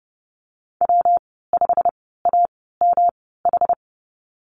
Morse code, heard as a keyed tone, about 31 wpm, spaced as for 17 wpm, 700 Hz. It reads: W5AMH
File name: Lesson 19g - Letter w - Call Signs 30wpm